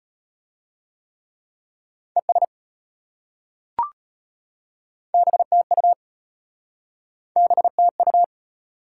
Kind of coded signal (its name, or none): Morse code